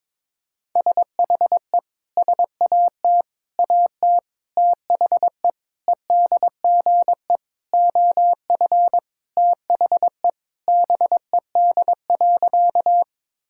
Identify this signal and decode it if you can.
Morse code, heard as a keyed tone, about 22 wpm, 700 Hz. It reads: SHE SAT AT THE EDGE OF THE BED.
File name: practice / 22wpm / Sentences with 4-Word Phrases - 22wpm